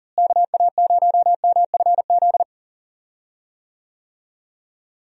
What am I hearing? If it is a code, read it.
Morse code, heard as a keyed tone, about 39 wpm, 700 Hz. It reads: KA0MFZ
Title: USA Call Signs - 6 Characters - Repeated 3x - 40wpm